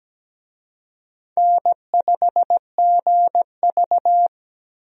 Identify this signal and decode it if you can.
Morse code, heard as a keyed tone, about 17 wpm, 700 Hz. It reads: N5GV
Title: USA Call Signs - 4 Characters - 17wpm